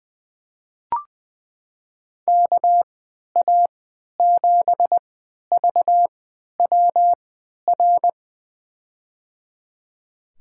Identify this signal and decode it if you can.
Morse code, heard as a keyed tone, about 20 wpm, spaced as for 11 wpm, 700 Hz. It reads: KA7VWR
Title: USA Call Signs - 6 Characters - 3x Character Spacing - 20wpm